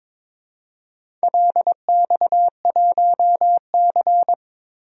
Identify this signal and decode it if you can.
Morse code, heard as a keyed tone, about 22 wpm, 700 Hz. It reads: LX1C
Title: DXCC-100 Call Signs - 4 Characters - Repeated 2x - 22wpm